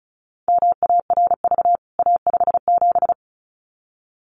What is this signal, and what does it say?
Morse code, heard as a keyed tone, about 34 wpm, 700 Hz. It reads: MARV A57
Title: CWT Exchange 35wpm